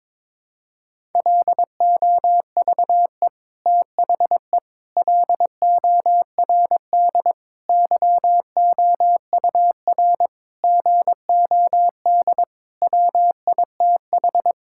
Morse code, heard as a keyed tone, about 22 wpm, 700 Hz. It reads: LOVE THE LORD YOUR GOD WITH
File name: Sets of 9 Words - 22wpm